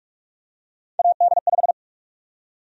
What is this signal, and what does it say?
Morse code, heard as a keyed tone, about 46 wpm, 700 Hz. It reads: AD5
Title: USA Call Sign Prefixes - Repeated 2x - 45wpm